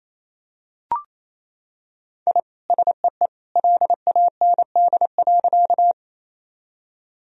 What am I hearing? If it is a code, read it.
Morse code, heard as a keyed tone, about 28 wpm, 700 Hz. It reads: I SEE LAND.
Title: Sentences from Top 1000 Words - Review Part 1 - 28wpm